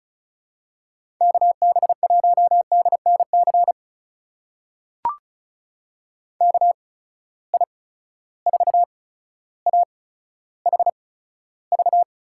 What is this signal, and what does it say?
Morse code, heard as a keyed tone, about 36 wpm, spaced as for 15 wpm, 700 Hz. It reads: KB1DNC KI4AHV
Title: USA Call Signs - 6 Characters - 8x Character Spacing - 35wpm